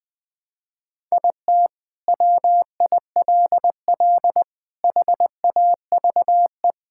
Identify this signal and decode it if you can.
Morse code, heard as a keyed tone, about 20 wpm, 700 Hz. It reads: IT WILL HAVE
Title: Sentences from Top 1000 Words - Review Part 6 - 20wpm